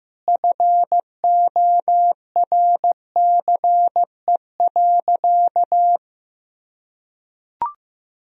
Morse code, heard as a keyed tone, about 15 wpm, 700 Hz. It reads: FORCE.
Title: Sentences from Top 1000 Words - Review Part 3 - 15wpm